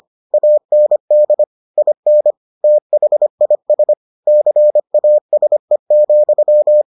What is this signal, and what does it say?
Morse code, heard as a keyed tone, about 25 wpm, 600 Hz. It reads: AND IN THIS CASE,